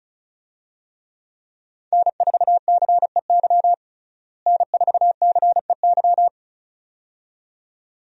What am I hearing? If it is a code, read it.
Morse code, heard as a keyed tone, about 34 wpm, 700 Hz. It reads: N4CEY N4CEY
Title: USA Call Signs - 5 Characters - Repeated 2x - 35wpm